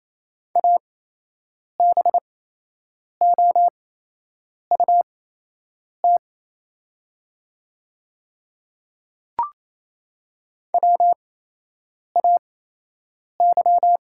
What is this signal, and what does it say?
Morse code, heard as a keyed tone, about 28 wpm, spaced as for 8 wpm, 700 Hz. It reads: ABOUT WAY
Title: 8x Character Spacing - Top 100 Words - 28wpm